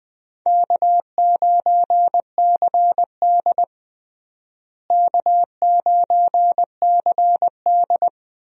Morse code, heard as a keyed tone, about 20 wpm, 700 Hz. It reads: K9CD K9CD
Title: USA Call Signs - 4 Characters - Repeated 2x - 20wpm